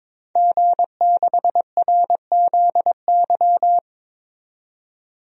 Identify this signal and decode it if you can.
Morse code, heard as a keyed tone, about 22 wpm, 700 Hz. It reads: G6RZY
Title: DXCC-100 Call Signs - 5 Characters - 22wpm